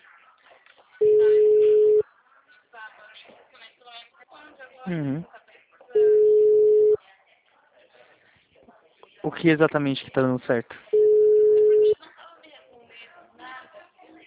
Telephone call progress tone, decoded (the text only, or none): ringback tone